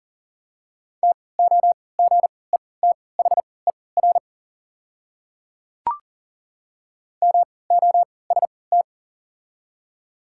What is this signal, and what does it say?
Morse code, heard as a keyed tone, about 40 wpm, spaced as for 23 wpm, 700 Hz. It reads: TOGETHER MOST
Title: 3x Character Spacing - Top 300 Words - 40wpm